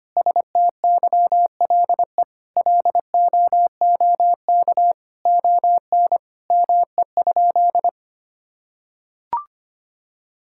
Morse code, heard as a keyed tone, about 25 wpm, 700 Hz. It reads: STYLE LOOK ON ME?
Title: Sentences from Top 1000 Words - Review Part 5 - 25wpm